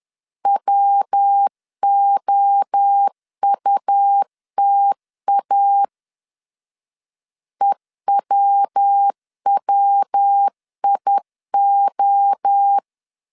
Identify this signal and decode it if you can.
Morse code, heard as a keyed tone, about 10 wpm, 800 Hz. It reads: WOUTA EWWIO
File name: CW / DET / 1 3 1500C S C 10wpm 3335